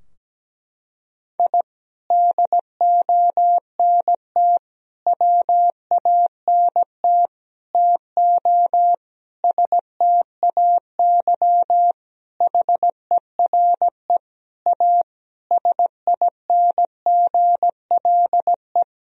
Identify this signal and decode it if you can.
Morse code, heard as a keyed tone, about 17 wpm, 700 Hz. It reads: I DONT WANT TO STAY HERE A SINGLE
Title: Sentences from Top 1000 Words - Review Part 4 - 17wpm